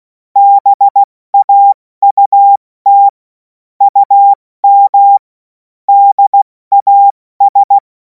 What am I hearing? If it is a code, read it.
Morse code, heard as a keyed tone, about 16 wpm, 800 Hz. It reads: BAUT UM DAS
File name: Morsekurs / Witze / Witz 14. April 2026 70 F80